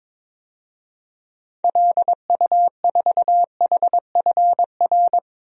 Morse code, heard as a keyed tone, about 22 wpm, 700 Hz. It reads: LU4HFR
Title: DXCC-100 Call Signs - 6 Characters - 22wpm